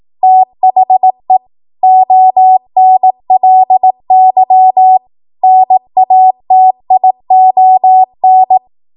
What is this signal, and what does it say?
Morse code, heard as a keyed tone, about 18 wpm, 750 Hz. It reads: THE ONLY NATION
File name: Morse / 18WPM